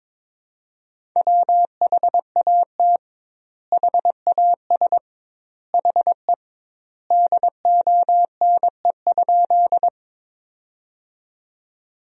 Morse code, heard as a keyed tone, about 22 wpm, 700 Hz. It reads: WHAT HAS HE DONE?
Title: 2x Word-Spacing - Sentences from Top 500 Words - 22wpm